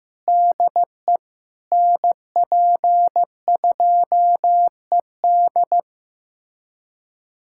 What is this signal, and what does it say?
Morse code, heard as a keyed tone, about 15 wpm, 700 Hz. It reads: DE NP2ED
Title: USA Callsign DE Callsign - 5 Characters - 15wpm